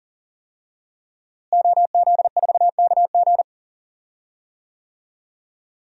Morse code, heard as a keyed tone, about 39 wpm, 700 Hz. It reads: OZ4KG
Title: DXCC-100 Call Signs - 5 Characters - Repeated 3x - 40wpm